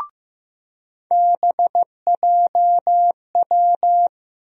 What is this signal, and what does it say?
Morse code, heard as a keyed tone, about 15 wpm, 700 Hz. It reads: BJW
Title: USA Call Sign Postfixes - 3 Letters 15wpm